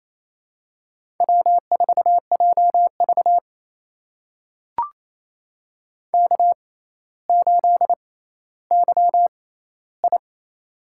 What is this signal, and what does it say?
Morse code, heard as a keyed tone, about 28 wpm, spaced as for 14 wpm, 700 Hz. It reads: W4JV K8YI
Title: USA Call Signs - 4 Characters - 6x Character Spacing - 28wpm